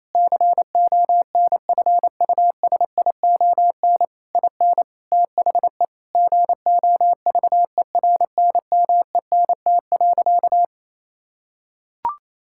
Morse code, heard as a keyed tone, about 28 wpm, 700 Hz. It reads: CONFUSION IN THE GOVERNMENT.
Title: Sentences from Top 1600 Words 28wpm